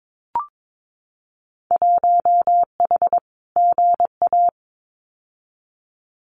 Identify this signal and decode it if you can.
Morse code, heard as a keyed tone, about 22 wpm, 700 Hz. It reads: 1H GA
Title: Winter Field Day Exchanges 22wpm